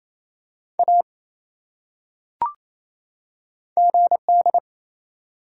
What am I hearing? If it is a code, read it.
Morse code, heard as a keyed tone, about 27 wpm, 700 Hz. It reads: A GD
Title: Top 180 QSO Elements 28wpm